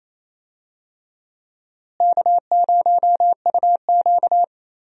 Morse code, heard as a keyed tone, about 28 wpm, 700 Hz. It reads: K0UQ
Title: USA Call Signs - 4 Characters - Repeated 3x - 28wpm